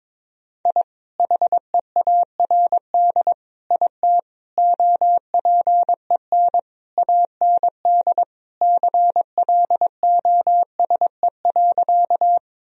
Morse code, heard as a keyed tone, about 22 wpm, 700 Hz. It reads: I HEARD IT OPEN AND CLOSE.
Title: Sentences from Top 1000 Words - Review Part 1 - 22wpm